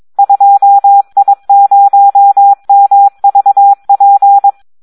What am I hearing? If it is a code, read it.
Morse code, heard as a keyed tone, about 22 wpm, 800 Hz. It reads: 2I0MVP
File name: Audio practica CW / SD 1 22